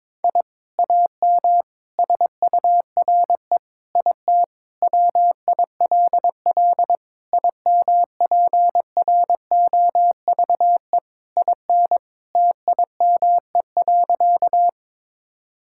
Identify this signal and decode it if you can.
Morse code, heard as a keyed tone, about 22 wpm, 700 Hz. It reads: I AM SURE IT WILL IMPROVE IN TIME.
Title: Sentences from Top 1000 Words - Review Part 5 - 22wpm